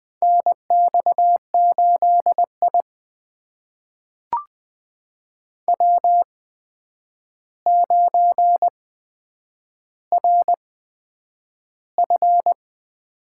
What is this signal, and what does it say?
Morse code, heard as a keyed tone, about 20 wpm, spaced as for 9 wpm, 700 Hz. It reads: NX8I W9RF
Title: USA Call Signs - 4 Characters - 8x Character Spacing - 20wpm